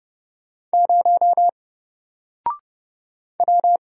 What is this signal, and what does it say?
Morse code, heard as a keyed tone, about 30 wpm, 700 Hz. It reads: 0 W
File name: Single-Letter-Number-30wpm